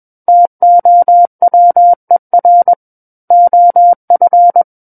Morse code, heard as a keyed tone, about 21 wpm, 700 Hz. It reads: TOWER OF
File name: Book One / 12-Chapter 12 - 21 WPM